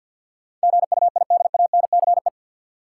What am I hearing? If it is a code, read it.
Morse code, heard as a keyed tone, about 51 wpm, 700 Hz. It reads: GUIDANCE
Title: Sentences from Top 1600 Words - New Word Drill Alt - 50wpm